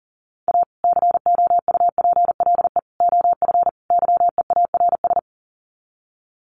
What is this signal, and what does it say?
Morse code, heard as a keyed tone, about 40 wpm, 700 Hz. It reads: A COUPLE OF YEARS
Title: Sets of 4 Words 40wpm